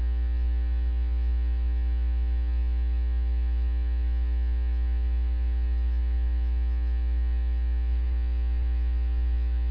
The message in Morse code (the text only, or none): none